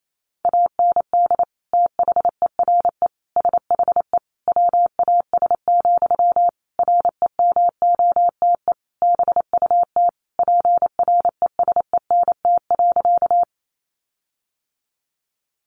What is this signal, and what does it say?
Morse code, heard as a keyed tone, about 28 wpm, 700 Hz. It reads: AND THERE SHE WAS, REMOTE BUT PRESENT.
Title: Sentences from Top 1300 Words 28wpm